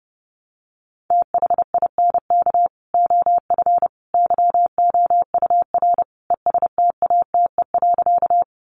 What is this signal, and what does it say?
Morse code, heard as a keyed tone, about 28 wpm, 700 Hz. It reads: THINK OF YOUR ESTATE.
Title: Sentences from Top 1200 Words 30wpm